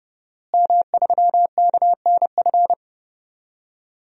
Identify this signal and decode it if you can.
Morse code, heard as a keyed tone, about 32 wpm, 700 Hz. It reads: M3KNF
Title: DXCC-100 Call Signs - 5 Characters - Repeated 2x - 30wpm